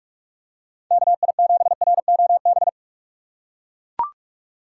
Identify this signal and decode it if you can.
Morse code, heard as a keyed tone, about 43 wpm, 700 Hz. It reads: KI7ROB